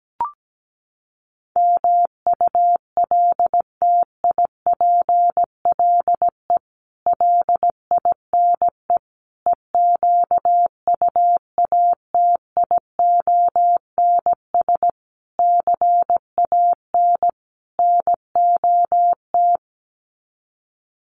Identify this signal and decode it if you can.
Morse code, heard as a keyed tone, about 17 wpm, 700 Hz. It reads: MULTIPLE LINE EQUATIONS CAN NOT
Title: Sets of 5 Words 17wpm